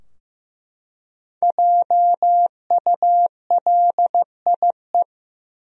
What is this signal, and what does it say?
Morse code, heard as a keyed tone, about 15 wpm, 700 Hz. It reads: JULIE
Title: USA Common Names - Max 5 Letters - 15wpm